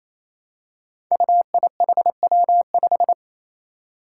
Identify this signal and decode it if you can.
Morse code, heard as a keyed tone, about 28 wpm, 700 Hz. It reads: UIHW5
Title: Five Challenging Letter-Number Combinations 28wpm